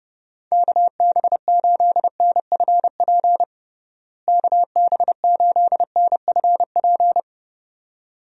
Morse code, heard as a keyed tone, about 30 wpm, 700 Hz. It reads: KB8NFP KB8NFP
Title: USA Call Signs - 6 Characters - Repeated 2x - 30wpm